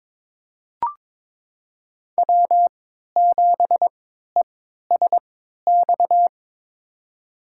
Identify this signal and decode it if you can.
Morse code, heard as a keyed tone, about 22 wpm, spaced as for 13 wpm, 700 Hz. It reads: W7ESX
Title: USA Call Signs - 5 Characters - 3x Character Spacing - 22wpm